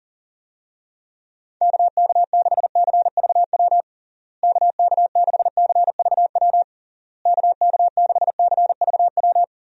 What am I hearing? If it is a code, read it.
Morse code, heard as a keyed tone, about 37 wpm, 700 Hz. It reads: KK6CVW KK6CVW KK6CVW
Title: USA Call Signs - 6 Characters - Repeated 3x - 40wpm